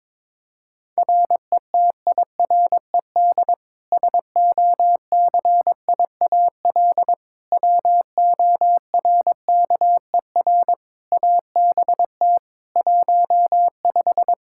Morse code, heard as a keyed tone, about 21 wpm, 700 Hz. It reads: RETIRED SOCIAL WORKER ABT 15
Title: Practice QSO - Exchange 3 - More Info - 22wpm